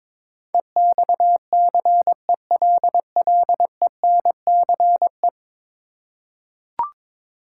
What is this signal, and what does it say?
Morse code, heard as a keyed tone, about 22 wpm, 700 Hz. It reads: EXCELLENCE
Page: morse-code-ninja-practice-22wpm